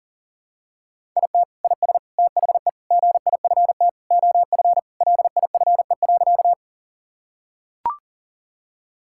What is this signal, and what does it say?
Morse code, heard as a keyed tone, about 40 wpm, 700 Hz. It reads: IT IS THE GIFT OF LIFE.
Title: Sentences from Top 1000 Words - Review Part 6 - 40wpm